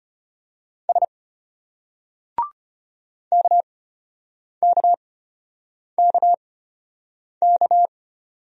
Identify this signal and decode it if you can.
Morse code, heard as a keyed tone, about 33 wpm, spaced as for 8 wpm, 700 Hz. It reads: S KKKK